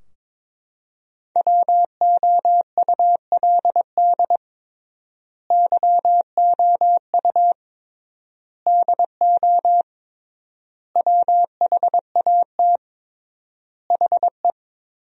Morse code, heard as a keyed tone, about 22 wpm, 700 Hz. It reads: WOULD YOU DO WHAT HE
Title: 3x Word-Spacing - Sentences from Top 100 Words - 22wpm